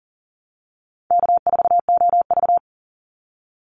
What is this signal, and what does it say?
Morse code, heard as a keyed tone, about 40 wpm, 700 Hz. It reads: K4OV
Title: Call Signs Repeated 2x - 40wpm